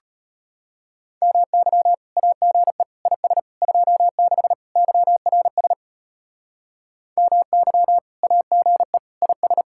Morse code, heard as a keyed tone, about 30 wpm, 700 Hz. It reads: MY AGE IS 26 YRS MY AGE IS
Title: Speed-Racing Common QSO Phrases - 25wpm